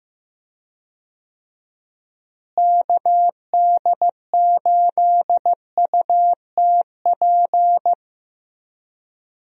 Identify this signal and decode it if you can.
Morse code, heard as a keyed tone, about 15 wpm, 700 Hz. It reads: KD8UTP
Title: USA Call Signs - 6 Characters - Repeated 2x - 15wpm